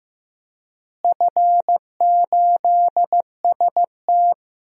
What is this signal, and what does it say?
Morse code, heard as a keyed tone, about 15 wpm, 700 Hz. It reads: F8ST
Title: DXCC-100 Call Signs - 4 Characters - Repeated 3x - 15wpm